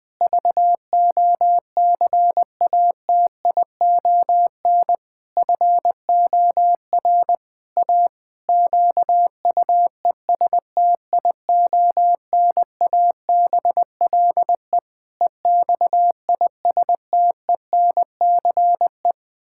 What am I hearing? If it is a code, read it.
Morse code, heard as a keyed tone, about 20 wpm, 700 Hz. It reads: VOCATION FOR A QUESTIONABLE EXISTENCE
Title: Pangram Challenge v2 - 20wpm